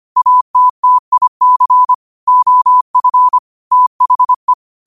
Morse code, heard as a keyed tone, about 25 wpm, 1000 Hz. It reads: ATTIC OF THE
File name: qotdfiles / 25wpm